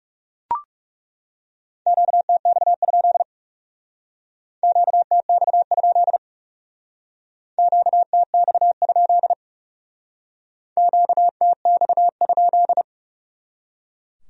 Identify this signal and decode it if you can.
Morse code, heard as a keyed tone, about 37 wpm, 700 Hz. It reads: QTX? QTX? QTX? QTX?
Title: Speed-Racing Q-Codes - 30wpm